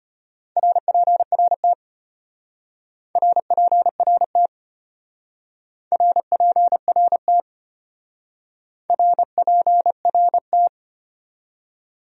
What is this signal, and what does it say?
Morse code, heard as a keyed tone, about 31 wpm, 700 Hz. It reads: RPRT RPRT RPRT RPRT
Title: Speed-Racing Abbreviations - 25wpm